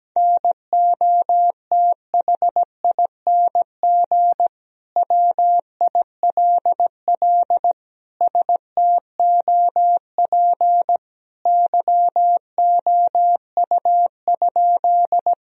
Morse code, heard as a keyed tone, about 17 wpm, 700 Hz. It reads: NOTHING WILL STOP YOU?
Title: Sentences from Top 1000 Words - Review Part 1 - 17wpm